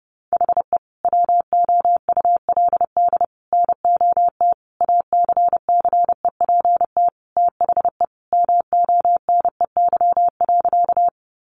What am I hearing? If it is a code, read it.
Morse code, heard as a keyed tone, about 28 wpm, 700 Hz. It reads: HE WOULD NOT ACCEPT THE MONEY.
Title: Sentences from Top 1300 Words 30wpm